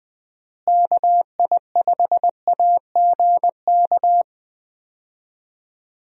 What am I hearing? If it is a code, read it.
Morse code, heard as a keyed tone, about 20 wpm, 700 Hz. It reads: KI5AGK
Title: USA Call Signs - 6 Characters - Repeated 2x - 20wpm